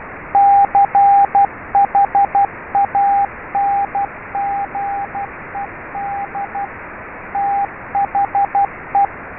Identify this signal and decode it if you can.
Morse code, heard as a keyed tone, about 12 wpm, 800 Hz, with radio noise. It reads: CHANGED THE
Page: MORSE CODE TRAINER AND SIMULATOR